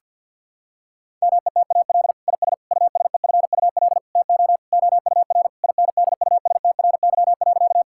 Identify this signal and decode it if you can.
Morse code, heard as a keyed tone, about 50 wpm, 700 Hz. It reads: METAL IS USEFUL TO OUR INDUSTRY.
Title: Sentences from Top 1000 Words - Review Part 6 - 50wpm